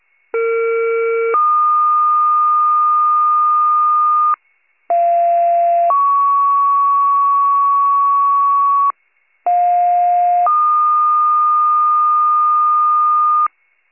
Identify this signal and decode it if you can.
Two-tone page, heard as a one-second tone, then a three-second tone, about 460 then 1220 Hz, about 690 then 1120 Hz, about 710 then 1220 Hz. It is heard from 0.3 to 13.5 s.